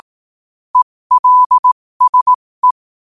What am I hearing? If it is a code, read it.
Morse code, heard as a keyed tone, about 18 wpm, spaced as for 15 wpm, 1000 Hz. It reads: ELSE
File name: qotdfiles / 15wpm